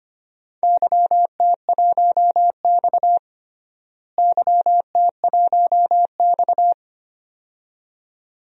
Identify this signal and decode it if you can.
Morse code, heard as a keyed tone, about 25 wpm, 700 Hz. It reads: YT1X YT1X
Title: DXCC-100 Call Signs - 4 Characters - Repeated 2x - 25wpm